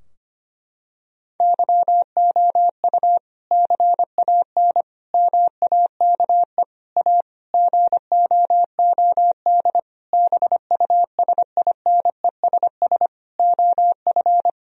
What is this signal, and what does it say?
Morse code, heard as a keyed tone, about 25 wpm, 700 Hz. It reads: YOU CAN MAKE A GOOD BUSINESS OF
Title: Sentences from Top 1000 Words - Review Part 4 - 25wpm